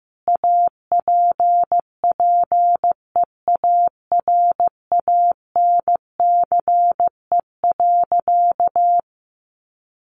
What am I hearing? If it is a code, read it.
Morse code, heard as a keyed tone, about 15 wpm, 700 Hz. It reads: APPEARANCE.
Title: Sentences from Top 1100 Words 15wpm